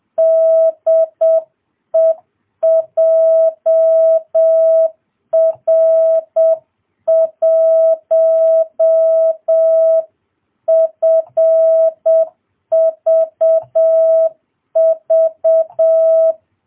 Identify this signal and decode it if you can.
Morse code, heard as a keyed tone, about 7 wpm, 650 Hz. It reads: DE JR1FVV